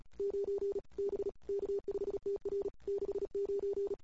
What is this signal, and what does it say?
Morse code, heard as a keyed tone, about 36 wpm, 410 Hz. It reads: 9 B K5TR 69
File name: CW / 2008-ss-cw-k5tr-op-n5rz -21:00